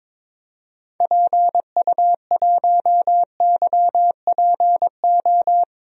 Morse code, heard as a keyed tone, about 21 wpm, 700 Hz. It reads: PU1YPO
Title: DXCC-100 Call Signs - 6 Characters - 22wpm